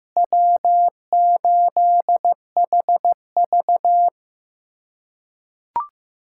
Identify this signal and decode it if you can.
Morse code, heard as a keyed tone, about 15 wpm, 700 Hz. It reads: W8HV
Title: USA Callsign DE Callsign - 4 Characters - 15wpm